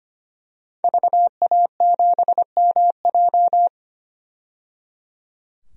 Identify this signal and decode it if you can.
Morse code, heard as a keyed tone, about 25 wpm, 700 Hz. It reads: VA7MJ
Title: DXCC-100 Call Signs - 5 Characters - 25wpm